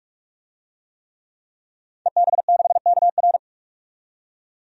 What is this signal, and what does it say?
Morse code, heard as a keyed tone, about 46 wpm, 700 Hz. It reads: EB6KR
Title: DXCC-100 Call Signs - 5 Characters - 45wpm